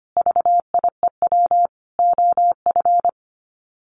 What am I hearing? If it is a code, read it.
Morse code, heard as a keyed tone, about 25 wpm, 700 Hz. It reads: VIEW OF
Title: Sets of 5 Words 25wpm